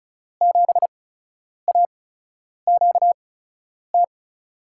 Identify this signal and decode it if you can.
Morse code, heard as a keyed tone, about 36 wpm, spaced as for 10 wpm, 700 Hz. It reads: 7AQT
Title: USA Call Signs - 5 Characters - 8x Character Spacing - 35wpm